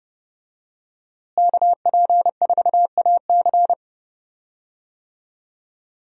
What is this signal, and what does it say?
Morse code, heard as a keyed tone, about 30 wpm, 700 Hz. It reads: KP4AC
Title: USA Call Signs - 5 Characters - Repeated 3x - 30wpm